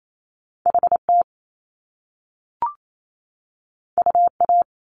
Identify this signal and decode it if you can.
Morse code, heard as a keyed tone, about 28 wpm, 700 Hz. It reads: HT UA
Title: Top Two Letter Combinations in Words 28wpm